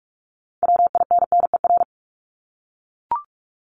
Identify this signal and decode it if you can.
Morse code, heard as a keyed tone, about 44 wpm, 700 Hz. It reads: WINNER
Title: Top 1751-2000 Words 45wpm